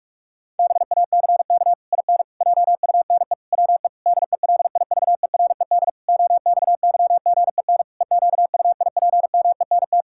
Morse code, heard as a keyed tone, about 46 wpm, 700 Hz. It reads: BACK IN JUNE WE DELIVERED OXYGEN EQUIPMENT
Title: Pangram Challenge v2 - 45wpm